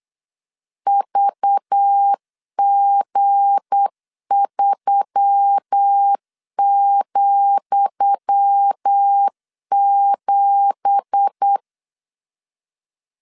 Morse code, heard as a keyed tone, about 8 wpm, 800 Hz. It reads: VG3,7
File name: LEZIONI / 10 SIGNS / 10D COMA 8wpm 3749